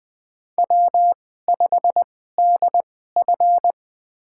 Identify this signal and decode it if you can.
Morse code, heard as a keyed tone, about 20 wpm, spaced as for 15 wpm, 700 Hz. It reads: W5DF
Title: USA Call Signs - 4 Characters - 2x Character Spacing - 20wpm